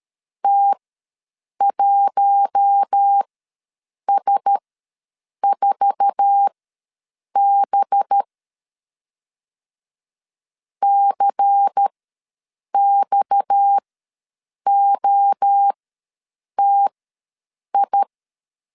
Morse code, heard as a keyed tone, about 13 wpm, spaced as for 7 wpm, 800 Hz. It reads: T1S4B CXOTI
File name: LEZIONI / 9 28 ws 12wpm 3354